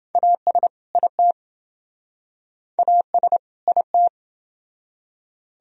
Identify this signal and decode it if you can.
Morse code, heard as a keyed tone, about 28 wpm, 700 Hz. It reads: AS IT AS IT